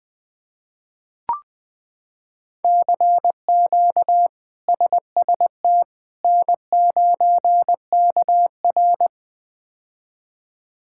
Morse code, heard as a keyed tone, about 20 wpm, 700 Hz. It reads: CQ SST N9KR
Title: K1USN SST QSO Practice - 20wpm